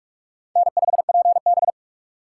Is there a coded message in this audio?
Yes: Morse code from 0.6 to 1.7 s, heard as a keyed tone, about 40 wpm, 700 Hz. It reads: N5PB